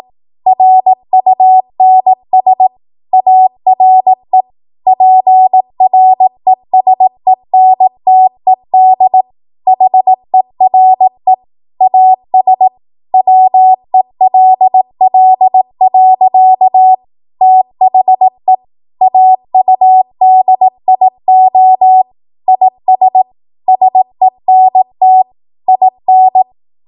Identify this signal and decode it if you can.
Morse code, heard as a keyed tone, about 18 wpm, 750 Hz. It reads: RUNS ARE PRESENTED HERE AS WELL. THE AUDIO IS SENT IN
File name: Archive / 18 WPM / 18WPM